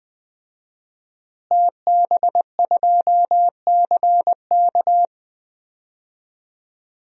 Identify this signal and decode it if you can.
Morse code, heard as a keyed tone, about 21 wpm, 700 Hz. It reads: TB2CK
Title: DXCC-100 Call Signs - 5 Characters - Repeated 2x - 20wpm